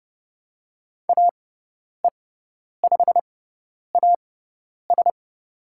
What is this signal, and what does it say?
Morse code, heard as a keyed tone, about 29 wpm, spaced as for 10 wpm, 700 Hz. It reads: AE5AS
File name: Lesson 16g - Letter c - Call Signs 30wpm